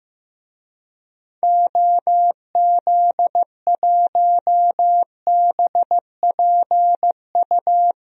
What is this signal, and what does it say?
Morse code, heard as a keyed tone, about 15 wpm, 700 Hz. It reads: OZ1BPU
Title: DXCC-100 Call Signs - 6 Characters - 15wpm